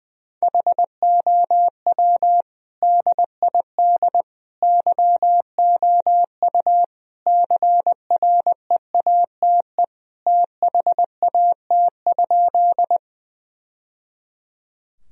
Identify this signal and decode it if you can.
Morse code, heard as a keyed tone, about 21 wpm, 700 Hz. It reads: HOW DID YOU CREATE THAT?
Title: Sentences from Top 1000 Words - Review Part 3 - 20wpm